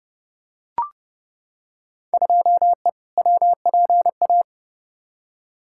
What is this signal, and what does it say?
Morse code, heard as a keyed tone, about 30 wpm, 700 Hz. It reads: 2E WPA
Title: Field Day Exchanges Encore - 30wpm